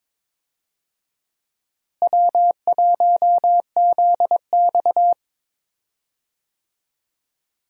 Morse code, heard as a keyed tone, about 22 wpm, 700 Hz. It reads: W1ZX
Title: USA Call Signs - 4 Characters - Repeated 2x - 22wpm